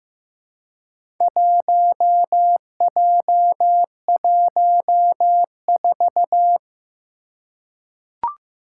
Morse code, heard as a keyed tone, about 15 wpm, 700 Hz. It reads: 1J14
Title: Four Challenging Letter-Number Combinations 15wpm